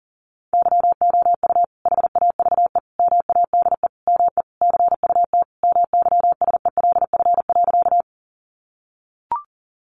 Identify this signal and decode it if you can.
Morse code, heard as a keyed tone, about 40 wpm, 700 Hz. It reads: YOU HAVE MADE ME CUT MYSELF.
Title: Sentences from Top 1300 Words 40wpm